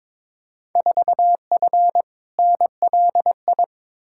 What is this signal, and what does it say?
Morse code, heard as a keyed tone, about 21 wpm, 700 Hz. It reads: 4F NLI
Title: Field Day Exchanges Encore - 22wpm